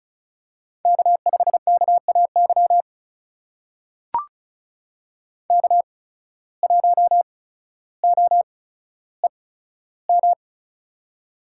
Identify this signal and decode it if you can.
Morse code, heard as a keyed tone, about 36 wpm, spaced as for 15 wpm, 700 Hz. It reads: K5KAY K1OEM
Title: USA Call Signs - 5 Characters - 8x Character Spacing - 35wpm